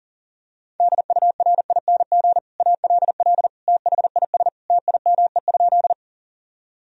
Morse code, heard as a keyed tone, about 40 wpm, 700 Hz. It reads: DURING ALL THIS TIME?
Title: Sentences from Top 1000 Words - Review Part 2 - 40wpm